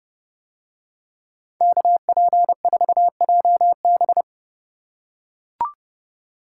Morse code, heard as a keyed tone, about 29 wpm, 700 Hz. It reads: KP4JB